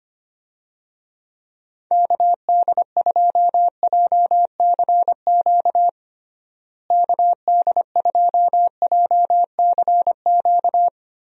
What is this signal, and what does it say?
Morse code, heard as a keyed tone, about 25 wpm, 700 Hz. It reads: KD2JCQ KD2JCQ
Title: USA Call Signs - 6 Characters - Repeated 3x - 25wpm